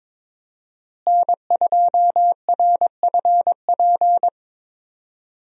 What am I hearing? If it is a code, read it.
Morse code, heard as a keyed tone, about 21 wpm, 700 Hz. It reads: N2RFP
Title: USA Call Signs - 5 Characters - Repeated 3x - 22wpm